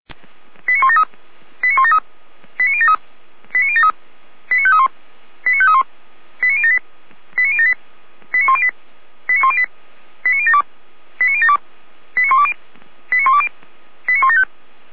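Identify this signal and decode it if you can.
Five-tone selective call (ZVEI), heard as quick runs of five tones, tones of about 70 ms. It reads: 89173 89173 89073 89073 89531 89531 89078 89078 89108 89108 89072 89072 89120 89120 89175
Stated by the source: ZVEI five-tone selective call received by SDR.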